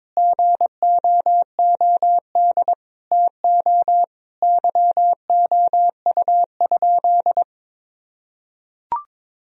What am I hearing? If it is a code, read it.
Morse code, heard as a keyed tone, about 22 wpm, 700 Hz. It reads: GOOD TO YOU?
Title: Sentences from Top 1000 Words - Review Part 4 - 22wpm